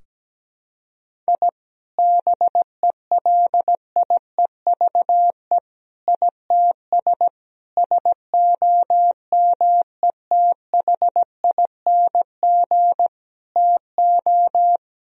Morse code, heard as a keyed tone, about 17 wpm, 700 Hz. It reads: I BELIEVE ITS SOMETHING TO
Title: Sentences from Top 1000 Words - Review Part 3 - 17wpm